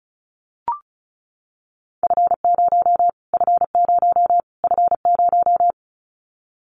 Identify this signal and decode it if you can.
Morse code, heard as a keyed tone, about 34 wpm, 700 Hz. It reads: F0 F0 F0
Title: Callsign Prefixes for Top 25 DXCC - Repeated 3x - 35wpm